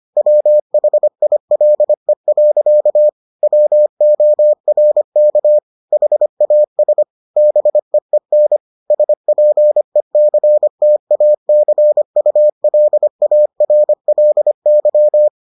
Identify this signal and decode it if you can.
Morse code, heard as a keyed tone, about 25 wpm, 600 Hz. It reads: WHILE. WORK HAS BEEN SPECTACULARLY